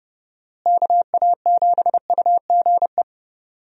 Morse code, heard as a keyed tone, about 32 wpm, 700 Hz. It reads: KA7UGE
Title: USA Call Signs - 6 Characters - Repeated 3x - 30wpm